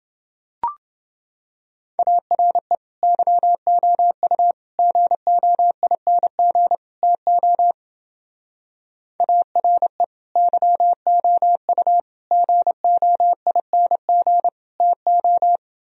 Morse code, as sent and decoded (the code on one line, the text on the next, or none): .- .-. / -.-- --- ..- / --. --- .. -. --. / - --- / .- .-. / -.-- --- ..- / --. --- .. -. --. / - ---
ARE YOU GOING TO ARE YOU GOING TO